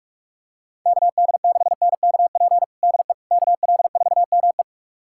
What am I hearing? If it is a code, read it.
Morse code, heard as a keyed tone, about 46 wpm, 690 Hz. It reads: KD6NKP DE KL4ME
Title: USA Callsign DE Callsign 45wpm